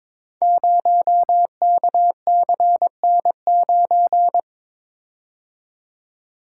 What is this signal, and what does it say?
Morse code, heard as a keyed tone, about 23 wpm, 700 Hz. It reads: 0KCN9
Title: Code Groups - 5 Letter:Numbers - 22wpm